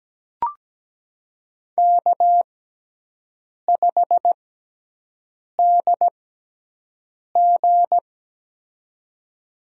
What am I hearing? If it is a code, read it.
Morse code, heard as a keyed tone, about 17 wpm, spaced as for 6 wpm, 700 Hz. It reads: K5DG
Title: USA Call Signs - 6x Character Spacing - 17wpm